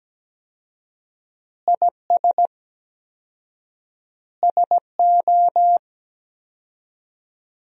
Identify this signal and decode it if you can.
Morse code, heard as a keyed tone, about 17 wpm, 700 Hz. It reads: IS SO
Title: 4x Word-Spacing - Sentences from Top 500 Words - 17wpm